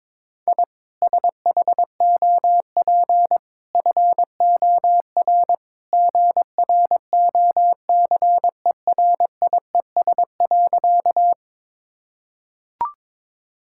Morse code, heard as a keyed tone, about 22 wpm, 700 Hz. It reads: I SHOP FOR GROCERIES.
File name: practice / 22wpm / Easy Sentences with Hints - 22wpm